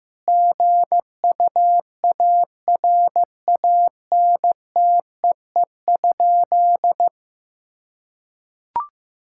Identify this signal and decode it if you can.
Morse code, heard as a keyed tone, about 15 wpm, 700 Hz. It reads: GUARANTEE?
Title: Sentences from Top 1600 Words 15wpm